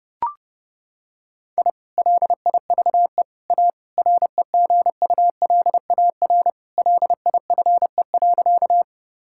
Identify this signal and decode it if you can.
Morse code, heard as a keyed tone, about 32 wpm, 700 Hz. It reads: I LIVE A REGULAR LIFE.
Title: Sentences from Top 1000 Words - Review Part 6 - 30wpm